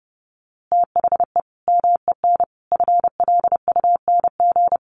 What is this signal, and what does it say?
Morse code, heard as a keyed tone, about 30 wpm, 700 Hz. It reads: THE MEN FLUNG
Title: A Sound Of Thunder 30wpm